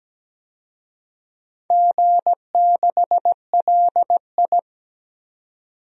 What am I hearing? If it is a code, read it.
Morse code, heard as a keyed tone, about 17 wpm, 700 Hz. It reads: G6LI
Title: DXCC-100 Call Signs - 4 Characters - 17wpm